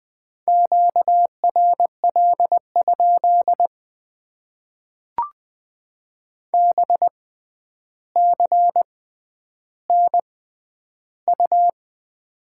Morse code, heard as a keyed tone, about 20 wpm, spaced as for 10 wpm, 700 Hz. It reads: QRL? BCNU
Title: Top 90 QSO Elements - 6x Character Spacing - 20wpm